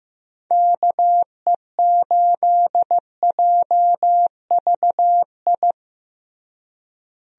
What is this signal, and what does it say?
Morse code, heard as a keyed tone, about 16 wpm, 700 Hz. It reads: KE8JVI
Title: USA Call Signs - 6 Characters - Repeated 2x - 15wpm